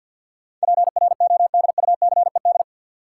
Morse code, heard as a keyed tone, about 52 wpm, 700 Hz. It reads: PRODUCED
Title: Sentences from Top 1000 Words - Review Part 6 - New Word Drill - 50wpm